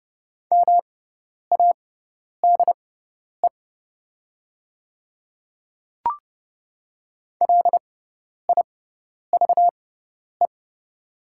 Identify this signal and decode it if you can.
Morse code, heard as a keyed tone, about 30 wpm, spaced as for 10 wpm, 700 Hz. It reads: MADE LIVE
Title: 6x Character Spacing - Top 200 Words - 30wpm